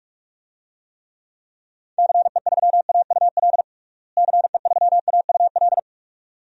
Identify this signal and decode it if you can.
Morse code, heard as a keyed tone, about 47 wpm, 700 Hz. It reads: CE3AUL CE3AUL
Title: DXCC-100 Call Signs - 6 Characters - Repeated 2x - 45wpm